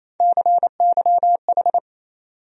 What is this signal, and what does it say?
Morse code, heard as a keyed tone, about 28 wpm, 700 Hz. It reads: CYH
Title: USA Call Sign Postfixes - 3 Letters 28wpm